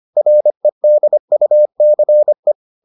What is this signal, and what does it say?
Morse code, heard as a keyed tone, about 25 wpm, 600 Hz. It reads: REDUCE